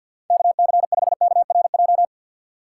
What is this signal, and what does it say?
Morse code, heard as a keyed tone, about 52 wpm, 700 Hz. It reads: KC5KRJ
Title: USA Call Signs - 6 Characters - 3x Character Spacing - 50wpm